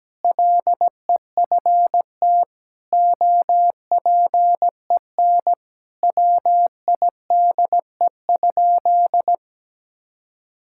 Morse code, heard as a keyed tone, about 17 wpm, 700 Hz. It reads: LEFT OPEN WIDE?
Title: Sentences from Top 1000 Words - Review Part 6 - 17wpm